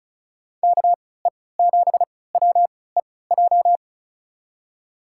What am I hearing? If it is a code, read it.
Morse code, heard as a keyed tone, about 34 wpm, spaced as for 20 wpm, 700 Hz. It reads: KE7WEJ
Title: USA Call Signs - 6 Characters - 3x Character Spacing - 35wpm